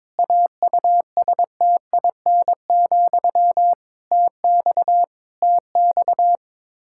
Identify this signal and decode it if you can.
Morse code, heard as a keyed tone, about 22 wpm, 700 Hz. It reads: AUSTIN, TX TX
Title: Practice QSO - Exchange 1 - Beginning - 22wpm